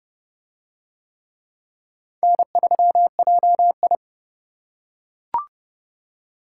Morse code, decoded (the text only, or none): N3JI